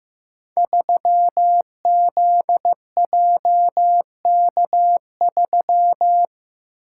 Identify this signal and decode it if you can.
Morse code, heard as a keyed tone, about 15 wpm, 700 Hz. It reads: 3ZJK3
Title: Code Groups - 5 Letter:Numbers - 15wpm